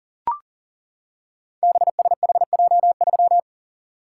Morse code, heard as a keyed tone, about 42 wpm, 690 Hz. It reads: BSHJ3
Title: Code Groups - 5 Letter:Numbers - 40wpm